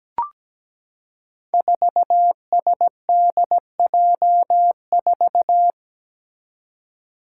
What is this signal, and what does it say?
Morse code, heard as a keyed tone, about 17 wpm, 700 Hz. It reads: 4SDJ4
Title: Five Challenging Letter-Number Combinations 17wpm